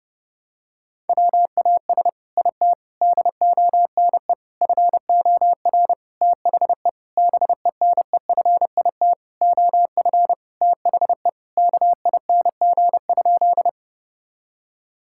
Morse code, heard as a keyed tone, about 30 wpm, 700 Hz. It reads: WAS IT DONE FOR THE BENEFIT OF THE KING?
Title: Sentences from Top 1000 Words - Review Part 6 - 30wpm